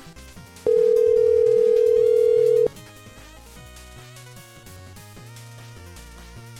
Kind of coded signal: telephone call progress tone